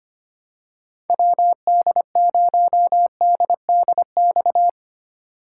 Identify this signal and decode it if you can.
Morse code, heard as a keyed tone, about 25 wpm, 700 Hz. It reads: WD0DDX
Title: USA Call Signs - 3x Character Spacing - 25wpm